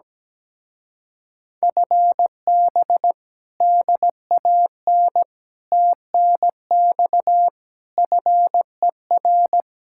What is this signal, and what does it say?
Morse code, heard as a keyed tone, about 17 wpm, 700 Hz. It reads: FB DAN TNX FER
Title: Speed-Racing Common QSO Phrases Encore - 17wpm